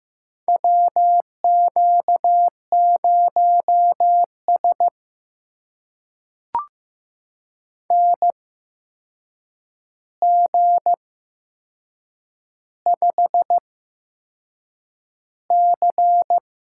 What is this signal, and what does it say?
Morse code, heard as a keyed tone, about 15 wpm, spaced as for 6 wpm, 700 Hz. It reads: WQ0S NG5C
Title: USA Call Signs - 4 Characters - 8x Character Spacing - 15wpm